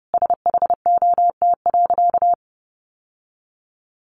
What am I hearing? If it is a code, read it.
Morse code, heard as a keyed tone, about 30 wpm, 700 Hz. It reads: SHOT.
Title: Sentences from Top 1100 Words 30wpm